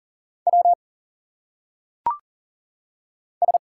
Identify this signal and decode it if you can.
Morse code, heard as a keyed tone, about 40 wpm, 700 Hz. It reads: W S
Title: Single Letters - Phonetic Pronunciation 40wpm